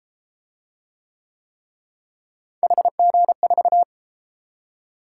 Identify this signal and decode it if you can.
Morse code, heard as a keyed tone, about 31 wpm, 700 Hz. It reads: HG4